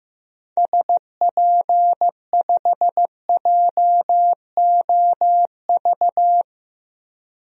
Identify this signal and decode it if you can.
Morse code, heard as a keyed tone, about 15 wpm, 700 Hz. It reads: SP5JOV
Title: DXCC-100 Call Signs - 6 Characters - 15wpm